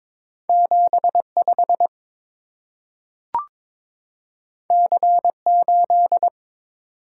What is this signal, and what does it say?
Morse code, heard as a keyed tone, about 23 wpm, 690 Hz. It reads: 75 C8
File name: practice / 22wpm / Two Letter-Number Combinations - 22wpm